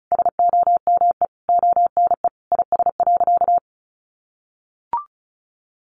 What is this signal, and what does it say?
Morse code, heard as a keyed tone, about 34 wpm, 700 Hz. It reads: SOMEONE IS.
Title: Sentences from Top 1100 Words 35wpm